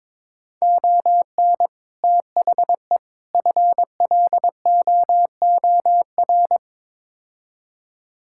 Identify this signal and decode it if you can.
Morse code, heard as a keyed tone, about 22 wpm, 700 Hz. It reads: ON THE FLOOR
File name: practice / 22wpm / Build A Phrase - 5-Word Phrases - 22wpm